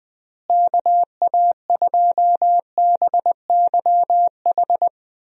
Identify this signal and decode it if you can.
Morse code, heard as a keyed tone, about 19 wpm, 700 Hz. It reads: KA2BYH
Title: USA Call Signs - 6 Characters - Repeated 2x - 20wpm